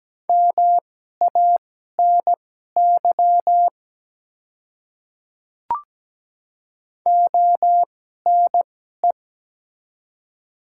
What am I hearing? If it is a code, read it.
Morse code, heard as a keyed tone, about 17 wpm, spaced as for 12 wpm, 700 Hz. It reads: MANY ONE
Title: 2x Character Spacing - Top 100 Words - 17wpm